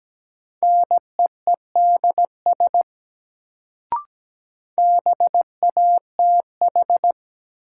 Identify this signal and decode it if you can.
Morse code, heard as a keyed tone, about 17 wpm, 700 Hz. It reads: NEEDS BATH
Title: Sentences from Top 1600 Words - New Word Drill Alt - 17wpm